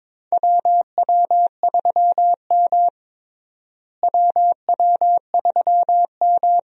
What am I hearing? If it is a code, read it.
Morse code, heard as a keyed tone, about 21 wpm, 700 Hz. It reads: WW3M WW3M
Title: USA Call Signs - 4 Characters - Repeated 2x - 22wpm